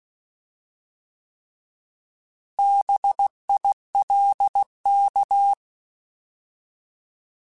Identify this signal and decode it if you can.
Morse code, heard as a keyed tone, about 16 wpm, 800 Hz. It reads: BILK
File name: Alphabetical / Alphabetical mp3s / Bs @15